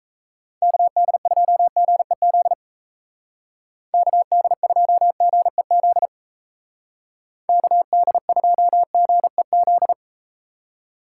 Morse code, heard as a keyed tone, about 37 wpm, 700 Hz. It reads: KD2GEZ KD2GEZ KD2GEZ